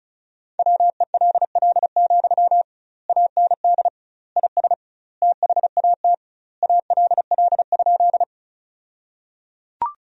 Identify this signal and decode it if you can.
Morse code, heard as a keyed tone, about 36 wpm, 700 Hz. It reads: WELL, AND IS THAT ALL?
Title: 2x Word-Spacing - Sentences from Top 200 Words - 35wpm